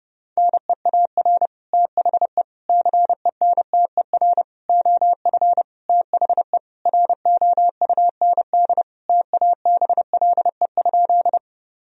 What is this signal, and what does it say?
Morse code, heard as a keyed tone, about 30 wpm, 700 Hz. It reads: NEAR THE CENTER OF THE ROUND TABLE?
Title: Sentences from Top 1000 Words - Review Part 2 - 30wpm